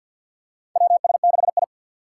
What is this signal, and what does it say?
Morse code, heard as a keyed tone, about 46 wpm, 700 Hz. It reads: WS6I
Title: USA Call Signs - 4 Characters - 3x Character Spacing - 50wpm